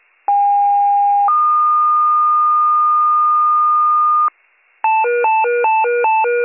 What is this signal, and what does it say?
Two-tone page, heard as a one-second tone, then a three-second tone, about 810 then 1220 Hz. It is heard from 0.3 to 4.3 s.